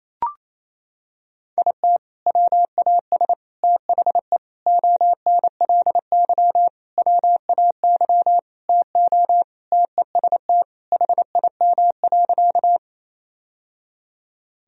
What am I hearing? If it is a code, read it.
Morse code, heard as a keyed tone, about 28 wpm, 700 Hz. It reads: IT WAS THE ONLY WAY TO TEST HIM.
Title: Sentences from Top 1000 Words - Review Part 3 - 28wpm